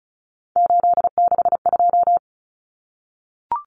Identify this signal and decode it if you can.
Morse code, heard as a keyed tone, about 34 wpm, 700 Hz. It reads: 862
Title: Numbers - 3-Digit 35wpm